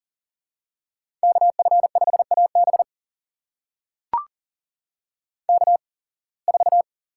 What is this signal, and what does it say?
Morse code, heard as a keyed tone, about 40 wpm, spaced as for 26 wpm, 700 Hz. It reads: KF5AB K4
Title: USA Call Signs - 8x Character Spacing - 40wpm